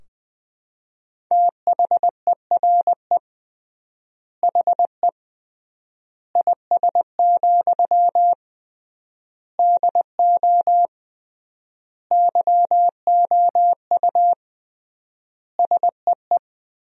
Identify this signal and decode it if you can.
Morse code, heard as a keyed tone, about 20 wpm, 700 Hz. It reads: THERE HE IS, DO YOU SEE
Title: 3x Word-Spacing - Sentences from Top 100 Words - 20wpm